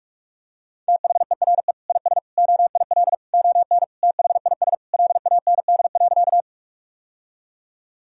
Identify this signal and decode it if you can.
Morse code, heard as a keyed tone, about 46 wpm, 700 Hz. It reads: THERE IS OIL ON THIS LAND.
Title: Sentences from Top 1000 Words - Review Part 4 - 45wpm